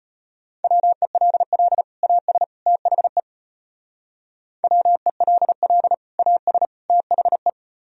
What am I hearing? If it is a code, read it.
Morse code, heard as a keyed tone, about 33 wpm, 700 Hz. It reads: WELL AS THE WELL AS THE